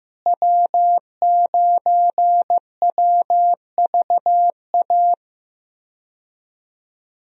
Morse code, heard as a keyed tone, about 15 wpm, 700 Hz. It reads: W9WVA
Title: USA Call Signs - 5 Characters - Repeated 2x - 15wpm